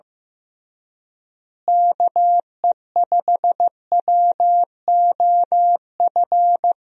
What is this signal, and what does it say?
Morse code, heard as a keyed tone, about 15 wpm, 700 Hz. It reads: KE5WOF
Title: USA Call Signs - 6 Characters - Repeated 3x - 15wpm